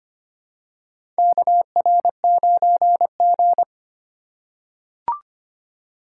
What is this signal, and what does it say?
Morse code, heard as a keyed tone, about 25 wpm, 700 Hz. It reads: KR9G
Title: USA Call Signs - 4x Character Spacing - 25wpm